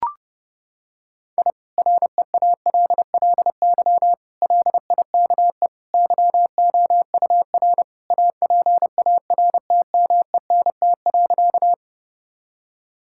Morse code, heard as a keyed tone, about 30 wpm, 700 Hz. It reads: I REALLY LIKE YOUR APARTMENT.
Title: Sentences from Top 1000 Words - Review Part 6 - 30wpm